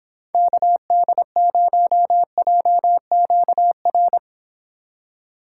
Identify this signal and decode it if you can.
Morse code, heard as a keyed tone, about 25 wpm, 700 Hz. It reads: KD0JQR